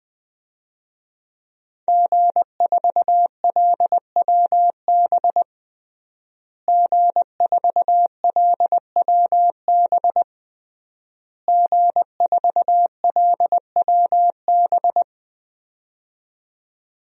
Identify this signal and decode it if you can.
Morse code, heard as a keyed tone, about 20 wpm, 700 Hz. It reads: G4LWB G4LWB G4LWB
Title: DXCC-100 Call Signs - 5 Characters - Repeated 3x - 20wpm